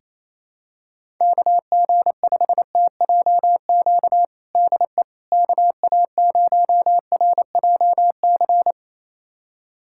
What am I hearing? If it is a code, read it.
Morse code, heard as a keyed tone, about 27 wpm, 700 Hz. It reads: KG5TJQ DE KA0RJC
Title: USA Callsign DE Callsign - 6 Characters - 28wpm